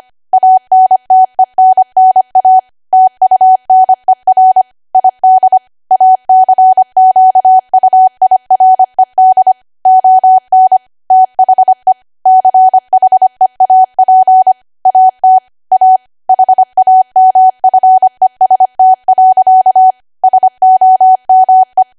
Morse code, heard as a keyed tone, about 25 wpm, 750 Hz. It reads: ANTENNA TUNER ID ACQUIRED ON THE CHEAP AT A HAMFEST. SOME
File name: Archive / 25 WPM / 25WPM